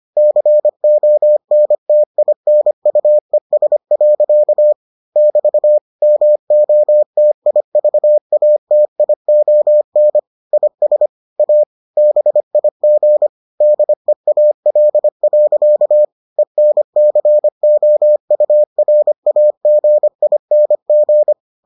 Morse code, heard as a keyed tone, about 25 wpm, 600 Hz. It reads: CONTINUES. <BT> MOTIVATION IS A BIG DEAL. ENCOURAGING